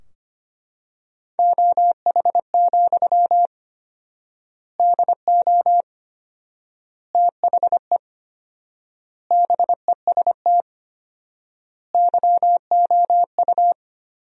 Morse code, heard as a keyed tone, about 25 wpm, 700 Hz. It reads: OH, DO THE BEST YOU
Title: 4x Word-Spacing - Sentences from Top 500 Words - 25wpm